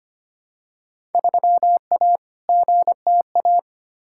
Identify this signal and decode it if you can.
Morse code, heard as a keyed tone, about 25 wpm, 700 Hz. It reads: 3A GTA
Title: Field Day Exchanges - 25wpm